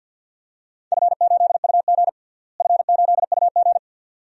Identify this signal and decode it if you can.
Morse code, heard as a keyed tone, about 48 wpm, 700 Hz. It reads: F8UG F8UG
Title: DXCC-100 Call Signs - 4 Characters - Repeated 2x - 50wpm